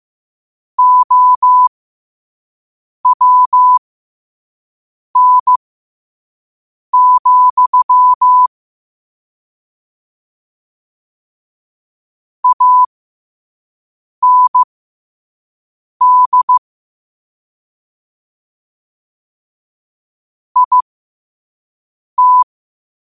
Morse code, heard as a keyed tone, about 15 wpm, spaced as for 5 wpm, 1000 Hz. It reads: OWN, AND IT
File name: qotdfiles / 5wpm